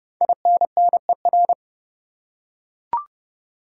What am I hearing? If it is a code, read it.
Morse code, heard as a keyed tone, about 30 wpm, 700 Hz. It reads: INNER
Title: Five Letter Words - 30wpm